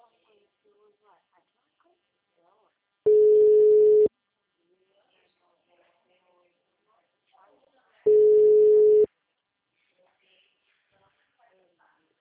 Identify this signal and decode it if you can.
Ringback tone (Europe, 425 Hz), with speech.